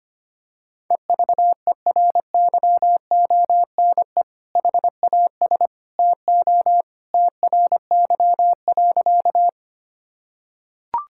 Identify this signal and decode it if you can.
Morse code, heard as a keyed tone, about 25 wpm, 700 Hz. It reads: EVERYONE HAS TO TRY.
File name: practice / 25wpm / Sentences with 2-Word Phrases - 25wpm